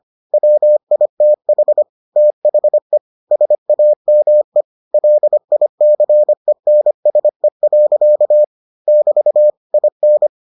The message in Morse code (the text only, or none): WITH THE SAME LICENSE. <BT> IN